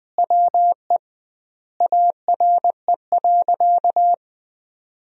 Morse code, heard as a keyed tone, about 21 wpm, 700 Hz. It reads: WE ARE.
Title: 2x Word-Spacing - Sentences from Top 200 Words - 20wpm